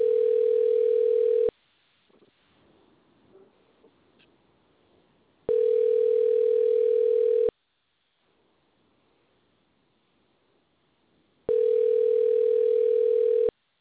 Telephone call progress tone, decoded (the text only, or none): ringback tone